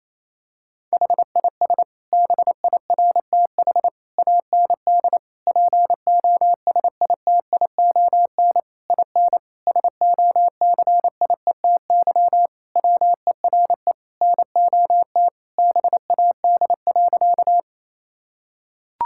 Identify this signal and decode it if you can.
Morse code, heard as a keyed tone, about 27 wpm, 700 Hz. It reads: HIS BIRTH AND POSITION IN SOCIETY WERE NOT BAD.
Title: Sentences from Top 1000 Words - Review Part 5 - 28wpm